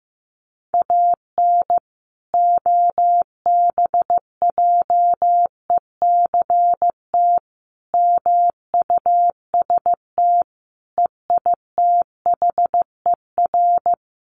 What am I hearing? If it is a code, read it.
Morse code, heard as a keyed tone, about 15 wpm, 700 Hz. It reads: AN OBJECT MUST EITHER
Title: Sentences from Top 1400 Words 15wpm